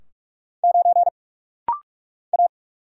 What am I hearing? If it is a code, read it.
Morse code, heard as a keyed tone, about 46 wpm, 700 Hz. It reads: 9 A
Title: Single Letter-Number - Rapid-Fire 45wpm